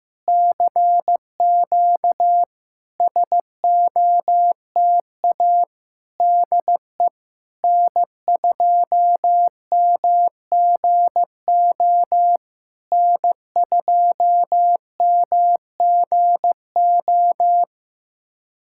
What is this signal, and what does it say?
Morse code, heard as a keyed tone, about 15 wpm, 700 Hz. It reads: CQ SOTA DE N2MGO N2MGO
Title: SOTA QSO Practice 15wpm